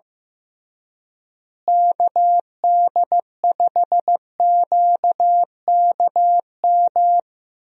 Morse code, heard as a keyed tone, about 15 wpm, 700 Hz. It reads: KD5QKM
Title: USA Call Signs - 6 Characters - Repeated 3x - 15wpm